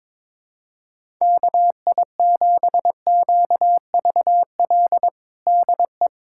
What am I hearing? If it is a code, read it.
Morse code, heard as a keyed tone, about 22 wpm, 700 Hz. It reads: KI7QVL DE
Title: USA Callsign DE Callsign 22wpm